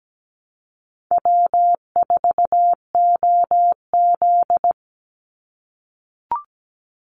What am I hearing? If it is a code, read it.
Morse code, heard as a keyed tone, about 17 wpm, 700 Hz. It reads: W4OZ
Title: USA Call Signs 17wpm